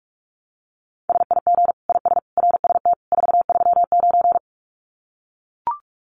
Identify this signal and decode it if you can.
Morse code, heard as a keyed tone, about 44 wpm, 700 Hz. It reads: SIG IS RST 439
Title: Common QSO Phrases 45wpm